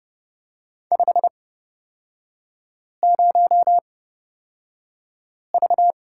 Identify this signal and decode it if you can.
Morse code, heard as a keyed tone, about 30 wpm, spaced as for 5 wpm, 700 Hz. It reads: 50V
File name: Lesson 37e - Letter x - Three Characters 30wpm